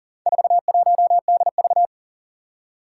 Morse code, heard as a keyed tone, about 40 wpm, 700 Hz. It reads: 41DV
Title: Four Challenging Letter-Number Combinations 40wpm